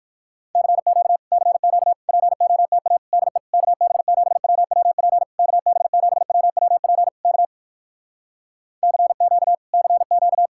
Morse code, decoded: CQ CQ POTA DE KB7WWP KB7WWP K CQ CQ

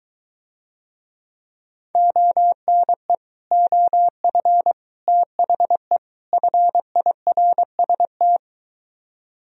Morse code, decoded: ONE OF THE FIRST

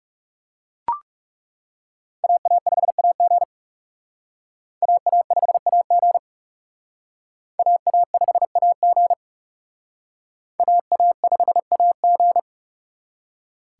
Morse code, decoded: AA5AG AA5AG AA5AG AA5AG

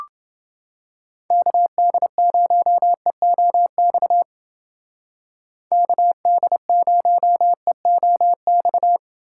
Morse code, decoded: KD0EOX KD0EOX

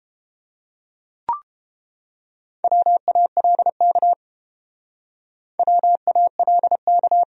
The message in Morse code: WALK WALK

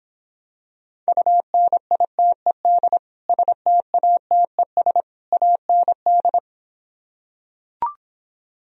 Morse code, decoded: UNITED STATES AND

